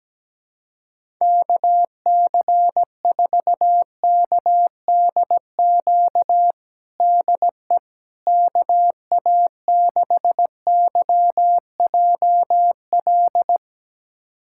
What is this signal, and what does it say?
Morse code, heard as a keyed tone, about 17 wpm, 700 Hz. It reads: KC4KDQ DE KA6YJL
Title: USA Callsign DE Callsign - 6 Characters - 17wpm